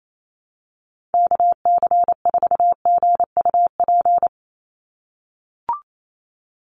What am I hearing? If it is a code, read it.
Morse code, heard as a keyed tone, about 28 wpm, 700 Hz. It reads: KC4GUP
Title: USA Call Signs 28wpm